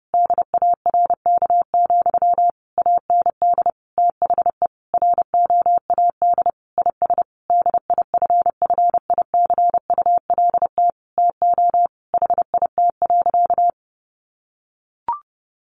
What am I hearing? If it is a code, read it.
Morse code, heard as a keyed tone, about 30 wpm, 700 Hz. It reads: DARK, AND THE ROAD IS DIFFICULT TO HIT.
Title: Sentences from Top 1200 Words 30wpm